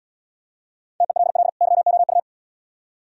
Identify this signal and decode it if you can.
Morse code, heard as a keyed tone, about 75 wpm, 700 Hz. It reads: TELL YOU